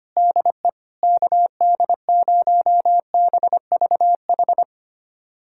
Morse code, heard as a keyed tone, about 25 wpm, 700 Hz. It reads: DE KD0BVH
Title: USA Callsign DE Callsign - 6 Characters - 25wpm